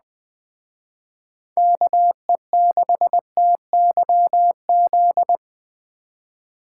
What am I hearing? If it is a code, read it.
Morse code, heard as a keyed tone, about 20 wpm, 700 Hz. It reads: KE6TYZ